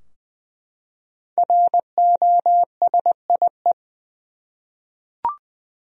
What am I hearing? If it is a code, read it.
Morse code, heard as a keyed tone, about 20 wpm, 700 Hz. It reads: ROSIE
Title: USA Common Names - Max 5 Letters - 20wpm